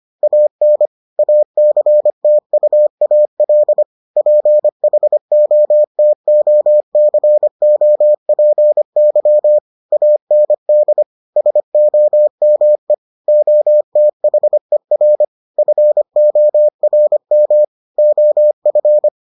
Morse code, heard as a keyed tone, about 25 wpm, 600 Hz. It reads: AN ACTUAL PHOTOCOPY AND SOME OTHER FORM OF